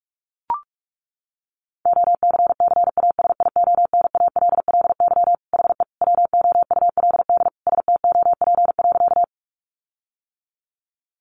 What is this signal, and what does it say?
Morse code, heard as a keyed tone, about 44 wpm, 700 Hz. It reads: OCCASIONALLY HE WOULD STOP.
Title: Sentences from Top 1300 Words 45wpm